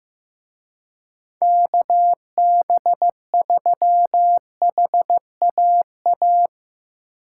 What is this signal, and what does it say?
Morse code, heard as a keyed tone, about 16 wpm, 700 Hz. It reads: KB3HAA
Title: Speed Racing - USA 2x3 Call Signs 15wpm